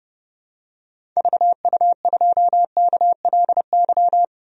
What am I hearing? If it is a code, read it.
Morse code, heard as a keyed tone, about 32 wpm, 700 Hz. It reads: VU2KLY
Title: DXCC-100 Call Signs - 6 Characters - 30wpm